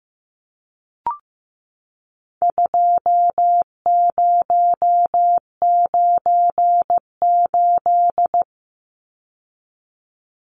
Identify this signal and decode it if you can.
Morse code, heard as a keyed tone, about 15 wpm, 700 Hz. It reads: 2098
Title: Numbers - 4-Digit 15wpm